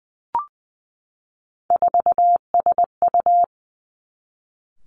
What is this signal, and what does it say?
Morse code, heard as a keyed tone, about 19 wpm, 700 Hz. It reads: 4SU
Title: Three Challenging Letter-Number Combinations 20wpm